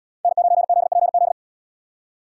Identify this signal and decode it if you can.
Morse code, heard as a keyed tone, about 74 wpm, 700 Hz. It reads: N9ZPZ